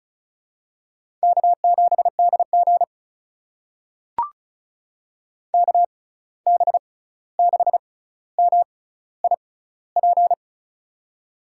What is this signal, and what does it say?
Morse code, heard as a keyed tone, about 36 wpm, spaced as for 16 wpm, 700 Hz. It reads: K7DG KB6MIP
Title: USA Call Signs - 6x Character Spacing - 35wpm